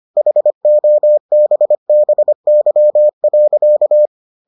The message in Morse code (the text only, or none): HOBBY.